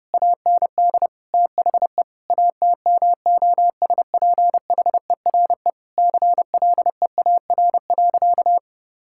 Morse code, heard as a keyed tone, about 30 wpm, 700 Hz. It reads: AND THE ATMOSPHERE CLEAR.
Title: Sentences from Top 1600 Words 30wpm